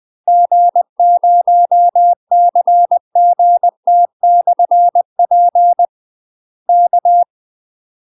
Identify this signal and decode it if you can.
Morse code, heard as a keyed tone, about 20 wpm, 700 Hz. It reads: G0CGT/P K
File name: UK MASTER QSOs TRAINING / 20WPM / UK MASTER QSOs 0046